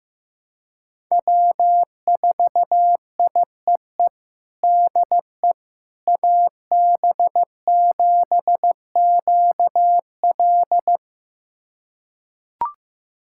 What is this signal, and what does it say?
Morse code, heard as a keyed tone, about 16 wpm, 700 Hz. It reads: W4IEE DE AB7QL
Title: USA Callsign DE Callsign - 5 Characters - 15wpm